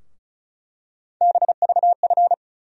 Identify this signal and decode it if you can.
Morse code, heard as a keyed tone, about 36 wpm, 700 Hz. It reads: BVF
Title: USA Call Sign Postfixes - 3 Letters 35wpm